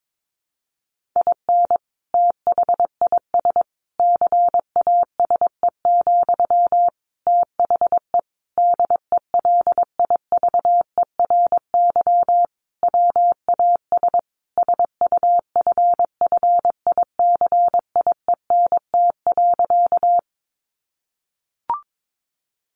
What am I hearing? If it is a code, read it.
Morse code, heard as a keyed tone, about 22 wpm, 700 Hz. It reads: IN THIS CASE, THE DELIVERY WAS SUFFICIENT.
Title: Sentences from Top 1500 Words 22wpm